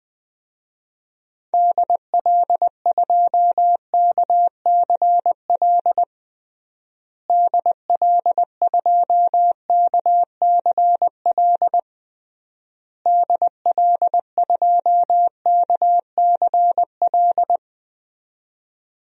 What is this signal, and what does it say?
Morse code, heard as a keyed tone, about 20 wpm, 700 Hz. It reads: DL2KCL DL2KCL DL2KCL
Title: DXCC-100 Call Signs - 6 Characters - Repeated 3x - 20wpm